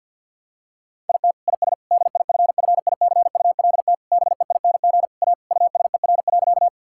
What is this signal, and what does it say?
Morse code, heard as a keyed tone, about 50 wpm, 700 Hz. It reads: IT IS DIFFICULT BEING A USER.
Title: Sentences from Top 1000 Words - Review Part 5 - 50wpm